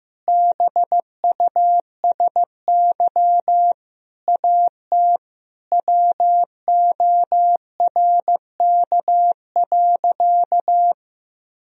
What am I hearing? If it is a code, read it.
Morse code, heard as a keyed tone, about 15 wpm, 700 Hz. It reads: BUSY AT WORK.
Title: Sentences from Top 1000 Words - Review Part 3 - 15wpm